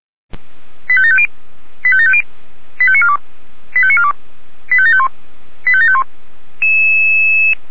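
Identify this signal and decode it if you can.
Five-tone selective call (ZVEI), heard as quick runs of five tones, tones of about 70 ms. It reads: 85750 85750 85942 85942 85751 85751
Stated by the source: Five-tone ZVEI selective call received on an SDR, recorded by Feuerwehr Orlamünde.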